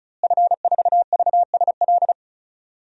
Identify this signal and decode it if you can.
Morse code, heard as a keyed tone, about 34 wpm, 700 Hz. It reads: F4VSL
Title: DXCC-100 Call Signs - 5 Characters - Repeated 2x - 35wpm